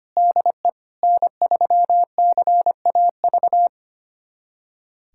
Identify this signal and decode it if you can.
Morse code, heard as a keyed tone, about 25 wpm, 700 Hz. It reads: DE N3CAV
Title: USA Callsign DE Callsign 25wpm